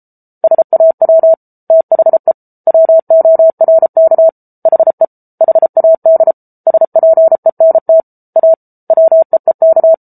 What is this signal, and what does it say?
Morse code, heard as a keyed tone, about 34 wpm, 650 Hz. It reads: SAW THE WORK HE HAD SPENT A WEEK
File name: Book Two / 07-Chapter 7 - 33 WPM